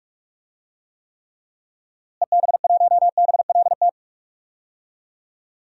Morse code, heard as a keyed tone, about 44 wpm, 700 Hz. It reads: EB1BLT
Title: DXCC-100 Call Signs - 6 Characters - Repeated 3x - 45wpm